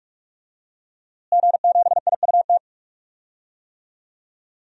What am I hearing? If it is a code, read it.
Morse code, heard as a keyed tone, about 46 wpm, 700 Hz. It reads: G7IUT
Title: DXCC-100 Call Signs - 5 Characters - Repeated 2x - 45wpm